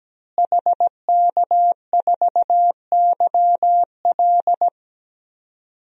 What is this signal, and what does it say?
Morse code, heard as a keyed tone, about 17 wpm, 700 Hz. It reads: HK4YL
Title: DXCC-100 Call Signs - 5 Characters - Repeated 3x - 17wpm